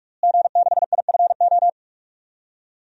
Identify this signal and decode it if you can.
Morse code, heard as a keyed tone, about 46 wpm, 700 Hz. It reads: G6IFO
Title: DXCC-100 Call Signs - 5 Characters - Repeated 3x - 45wpm